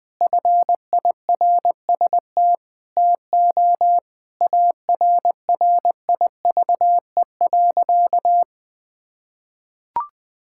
Morse code, heard as a keyed tone, about 20 wpm, 700 Hz. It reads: FIRST TO ARRIVE.
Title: Sentences with 3-Word Phrases - 20wpm